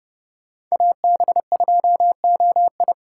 Morse code, heard as a keyed tone, about 31 wpm, 700 Hz. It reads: AB2OI